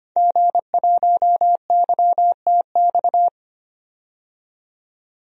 Morse code, heard as a keyed tone, about 25 wpm, 700 Hz. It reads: G1YTX
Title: DXCC-100 Call Signs - 5 Characters - 25wpm